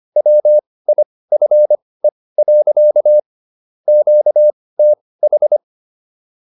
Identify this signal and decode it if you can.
Morse code, heard as a keyed tone, about 25 wpm, spaced as for 18 wpm, 600 Hz. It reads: WIFE. QTH